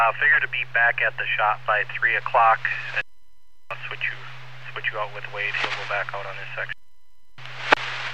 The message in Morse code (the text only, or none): none